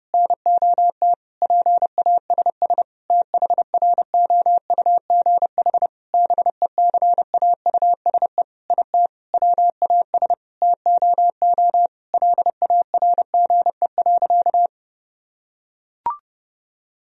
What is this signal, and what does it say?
Morse code, heard as a keyed tone, about 30 wpm, 700 Hz. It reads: NOT PASS THROUGH BECAUSE IT WAS TOO LARGE.
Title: Sentences from Top 1000 Words - Review Part 1 - 30wpm